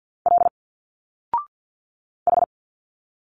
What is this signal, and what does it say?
Morse code, heard as a keyed tone, about 50 wpm, 700 Hz. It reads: L H
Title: Single Letters 50wpm